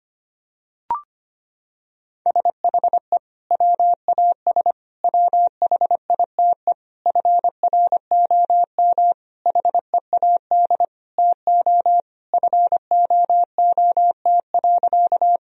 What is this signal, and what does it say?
Morse code, heard as a keyed tone, about 25 wpm, 700 Hz. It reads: SHE WAS WHITE FROM HEAD TO FOOT.
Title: Sentences from Top 1000 Words - Review Part 3 - 25wpm